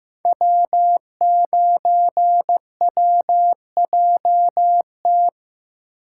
Morse code, heard as a keyed tone, about 15 wpm, 700 Hz. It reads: W9WJT
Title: Code Groups - 5 Letter:Numbers - 15wpm